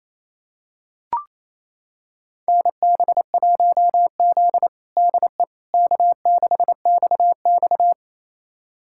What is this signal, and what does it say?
Morse code, heard as a keyed tone, about 28 wpm, 700 Hz. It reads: NB1Z DE K6XX
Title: USA Callsign DE Callsign - 4 Characters - 28wpm